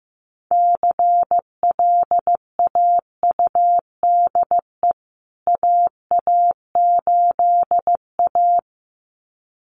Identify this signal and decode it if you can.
Morse code, heard as a keyed tone, about 15 wpm, 700 Hz. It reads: CLAUDE AA8A
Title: CWT Exchange 15wpm